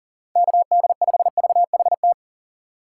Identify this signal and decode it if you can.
Morse code, heard as a keyed tone, about 40 wpm, 700 Hz. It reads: KD5VHT
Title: USA Call Signs - 6 Characters - 4x Character Spacing - 40wpm